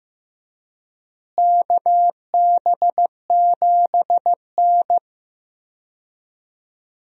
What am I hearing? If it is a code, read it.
Morse code, heard as a keyed tone, about 15 wpm, 700 Hz. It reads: KB7N
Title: USA Call Signs - 4 Characters - Repeated 3x - 15wpm